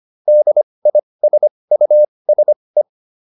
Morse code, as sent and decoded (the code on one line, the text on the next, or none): -.. .. ... ..- ... .
DISUSE